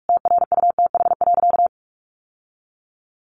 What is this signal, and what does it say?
Morse code, heard as a keyed tone, about 44 wpm, 700 Hz. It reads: TRUTH.
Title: Sentences from Top 1100 Words 45wpm